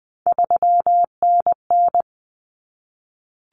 Morse code, heard as a keyed tone, about 20 wpm, 700 Hz. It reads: 3NN
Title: Common Cut Numbers - 3-Digit 20wpm